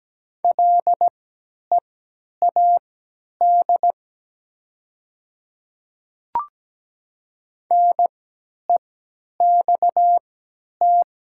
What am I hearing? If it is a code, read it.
Morse code, heard as a keyed tone, about 17 wpm, spaced as for 10 wpm, 690 Hz. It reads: LEAD NEXT
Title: 3x Character Spacing - Top 500 Words - 17wpm